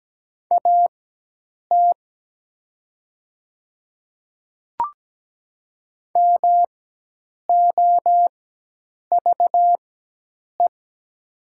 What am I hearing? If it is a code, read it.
Morse code, heard as a keyed tone, about 17 wpm, spaced as for 8 wpm, 700 Hz. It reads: AT MOVE
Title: 4x Character Spacing - Top 200 Words - 17wpm